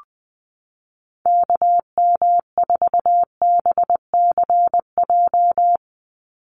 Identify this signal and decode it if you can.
Morse code, heard as a keyed tone, about 20 wpm, 700 Hz. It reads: KM4BCJ
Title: USA Call Signs 20wpm